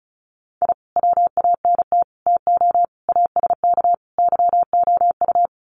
Morse code, heard as a keyed tone, about 34 wpm, 700 Hz. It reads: I WANT TO ASK YOU
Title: Sets of 5 Words 35wpm